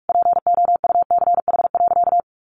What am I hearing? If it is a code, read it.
Morse code, heard as a keyed tone, about 46 wpm, 700 Hz. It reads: POUCH.
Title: Easy Phrases 45wpm